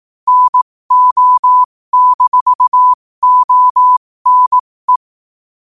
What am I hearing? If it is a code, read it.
Morse code, heard as a keyed tone, about 18 wpm, spaced as for 15 wpm, 1000 Hz. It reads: NO-ONE
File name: qotdfiles / 15wpm